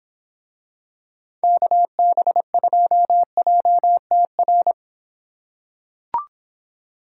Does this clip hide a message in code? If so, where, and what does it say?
Yes: Morse code from 1.4 to 4.7 s, heard as a keyed tone, about 24 wpm, 700 Hz. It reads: KB2JTR